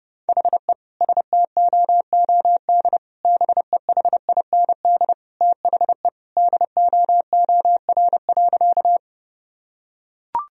Morse code, heard as a keyed tone, about 30 wpm, 700 Hz. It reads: HE STOOD BEHIND THE DOOR.
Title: Sentences from Top 1000 Words - Review Part 2 - 30wpm